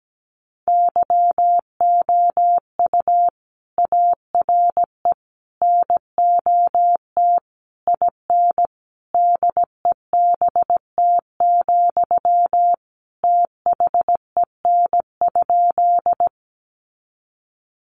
Morse code, heard as a keyed tone, about 17 wpm, 700 Hz. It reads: YOU ARE NOT IN DEBT, THEN?
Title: Sentences from Top 1300 Words 17wpm